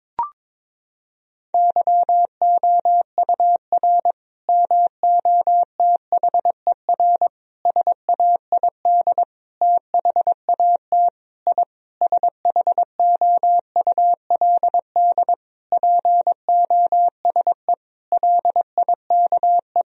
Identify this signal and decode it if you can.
Morse code, heard as a keyed tone, about 22 wpm, 700 Hz. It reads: YOUR MOTHER SAID THAT I SHOULD POSE LIKE
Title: Sentences from Top 1000 Words - Review Part 2 - 22wpm